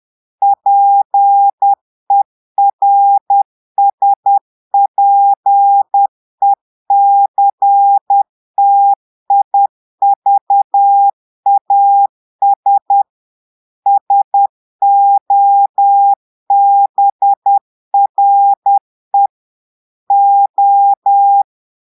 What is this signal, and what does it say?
Morse code, heard as a keyed tone, about 10 wpm, 800 Hz. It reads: PERSPECTIVAS SOBRE O
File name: Portugais - texte en clair 1 - vitesse - 10 mots minute